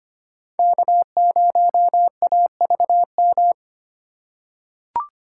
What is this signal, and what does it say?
Morse code, heard as a keyed tone, about 25 wpm, 700 Hz. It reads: K0AVM
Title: USA Call Signs - 5 Characters - 4x Character Spacing - 25wpm